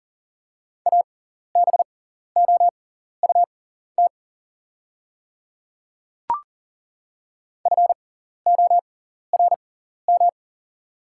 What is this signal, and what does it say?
Morse code, heard as a keyed tone, about 40 wpm, spaced as for 14 wpm, 700 Hz. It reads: ABOUT FORM
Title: 6x Character Spacing - Top 300 Words - 40wpm